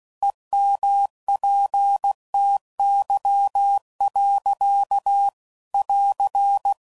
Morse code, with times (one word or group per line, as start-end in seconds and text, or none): 0.2-5.3 EMPTY.
5.7-6.7 <AR>